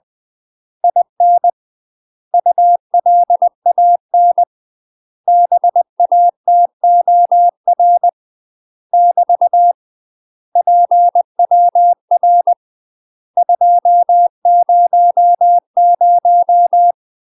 Morse code, heard as a keyed tone, about 20 wpm, 700 Hz. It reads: IN ULAN BATOR <BT> PWR 200
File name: UK MASTER QSOs TRAINING / 20WPM / UK MASTER QSOs 0032